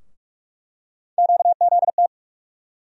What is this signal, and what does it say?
Morse code, heard as a keyed tone, about 42 wpm, 700 Hz. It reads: QZT
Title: USA Call Sign Postfixes - 3 Letters 45wpm